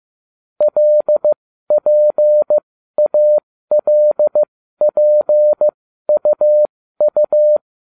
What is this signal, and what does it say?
Morse code, heard as a keyed tone, about 15 wpm, spaced as for 12 wpm, 600 Hz. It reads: LPALPUU